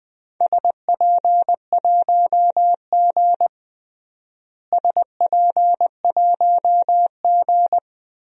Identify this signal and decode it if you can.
Morse code, heard as a keyed tone, about 21 wpm, 700 Hz. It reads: SP1G SP1G
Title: DXCC-100 Call Signs - 4 Characters - Repeated 2x - 20wpm